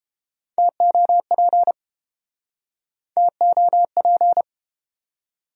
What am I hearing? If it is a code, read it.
Morse code, heard as a keyed tone, about 33 wpm, 700 Hz. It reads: TOP TOP